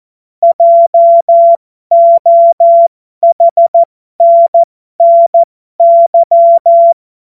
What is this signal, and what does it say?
Morse code, heard as a keyed tone, about 14 wpm, spaced as for 12 wpm, 680 Hz. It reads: JOHNNY
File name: Morsekurs / Witze